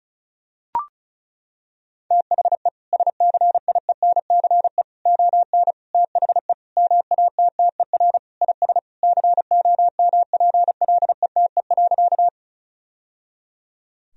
Morse code, heard as a keyed tone, about 34 wpm, 700 Hz. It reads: THE SCIENCE ON THE MATTER IS COMPLETE.
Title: Sentences from Top 1000 Words - Review Part 2 - 35wpm